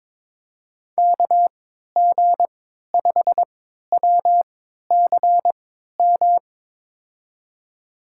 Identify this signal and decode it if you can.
Morse code, heard as a keyed tone, about 22 wpm, spaced as for 12 wpm, 700 Hz. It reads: KG5WCM
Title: USA Call Signs - 3x Character Spacing - 22wpm